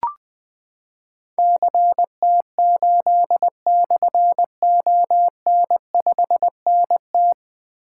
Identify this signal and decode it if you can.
Morse code, heard as a keyed tone, about 21 wpm, 700 Hz. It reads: CT8/ON5NT
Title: Difficult Call Signs 20wpm